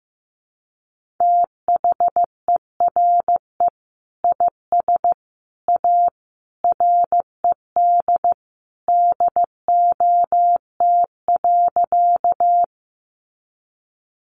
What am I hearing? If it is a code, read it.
Morse code, heard as a keyed tone, about 15 wpm, 700 Hz. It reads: THERE IS A RED DOT.
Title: Sentences from Top 1500 Words 15wpm